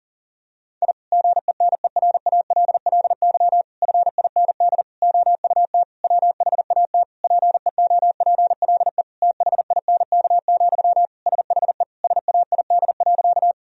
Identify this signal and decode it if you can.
Morse code, heard as a keyed tone, about 40 wpm, 690 Hz. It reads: I GENERALLY FIND OUT WHAT PEOPLE THINK, SHE SAID.
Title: Sentences from Top 1000 Words - Review Part 4 - 40wpm